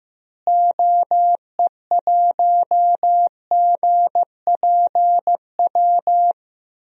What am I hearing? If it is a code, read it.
Morse code, heard as a keyed tone, about 15 wpm, 700 Hz. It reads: OE1GPW
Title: DXCC-100 Call Signs - 6 Characters - Repeated 3x - 15wpm